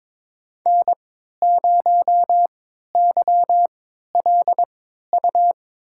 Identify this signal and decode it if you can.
Morse code, heard as a keyed tone, about 22 wpm, spaced as for 12 wpm, 700 Hz. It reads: N0YLU
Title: USA Call Signs - 5 Characters - 3x Character Spacing - 22wpm